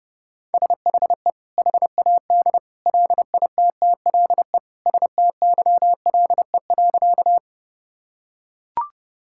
Morse code, heard as a keyed tone, about 30 wpm, 700 Hz. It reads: SHE HAD LITTLE STYLE.
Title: Sentences from Top 1000 Words - Review Part 5 - 30wpm